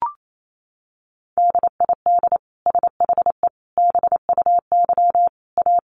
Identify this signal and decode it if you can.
Morse code, heard as a keyed tone, about 28 wpm, 700 Hz. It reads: DID SHE BUY A
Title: Sentences from Top 1300 Words 28wpm